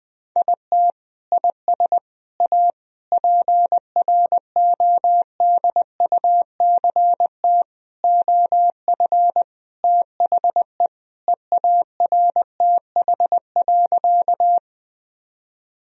Morse code, heard as a keyed tone, about 20 wpm, 700 Hz. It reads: IT IS A PRODUCT OF THE EARTH.
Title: Sentences from Top 1000 Words - Review Part 2 - 20wpm